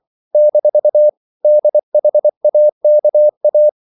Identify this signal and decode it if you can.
Morse code, heard as a keyed tone, about 24 wpm, 600 Hz. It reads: - DHAKA
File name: CW / Practice / NationalCapitals24